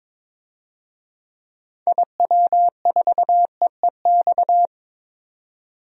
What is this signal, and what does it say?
Morse code, heard as a keyed tone, about 22 wpm, 700 Hz. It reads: IW4EEX
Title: DXCC-100 Call Signs - 6 Characters - 22wpm